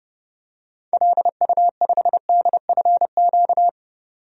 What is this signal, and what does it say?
Morse code, heard as a keyed tone, about 32 wpm, 700 Hz. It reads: LU5DFQ
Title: DXCC-100 Call Signs - 6 Characters - 30wpm